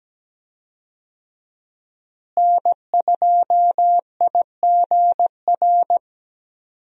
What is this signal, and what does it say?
Morse code, heard as a keyed tone, about 17 wpm, 700 Hz. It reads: N2IGR
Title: USA Call Signs - 5 Characters - Repeated 2x - 17wpm